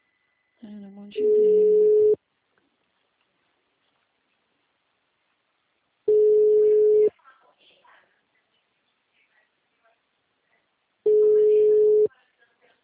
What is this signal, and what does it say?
Ringback tone (Europe, 425 Hz), with speech.
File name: out- -2008- - - .30534